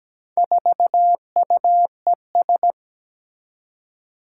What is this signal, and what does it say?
Morse code, heard as a keyed tone, about 17 wpm, 700 Hz. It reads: 4UES
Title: Four Challenging Letter-Number Combinations 17wpm